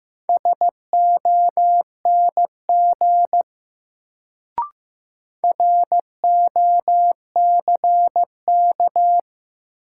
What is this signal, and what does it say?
Morse code, heard as a keyed tone, about 15 wpm, 700 Hz. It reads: SONG ROCK
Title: Sentences from Top 400 Words - New Word Drill - 15wpm